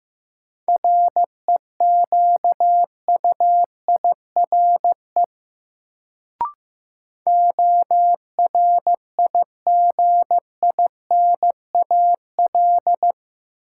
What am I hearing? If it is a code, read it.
Morse code, heard as a keyed tone, about 15 wpm, 700 Hz. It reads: REQUIRE ORIGINAL
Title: Sentences from Top 900 Words - New Word Drill - 15wpm